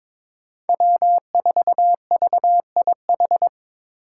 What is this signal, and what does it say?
Morse code, heard as a keyed tone, about 23 wpm, 700 Hz. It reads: W4VIH
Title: Code Groups - 5 Letter:Numbers - 22wpm